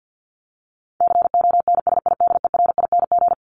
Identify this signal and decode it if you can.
Morse code, heard as a keyed tone, about 50 wpm, 700 Hz. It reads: CONSIDERING
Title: ING Suffix - Top 100 Words with Base Words 50wpm